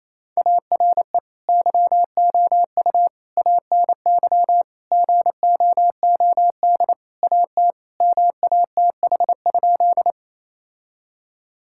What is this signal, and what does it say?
Morse code, heard as a keyed tone, about 28 wpm, 700 Hz. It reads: ARE YOU ANY GOOD AT MATH?
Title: Sentences from Top 1000 Words - Review Part 6 - 28wpm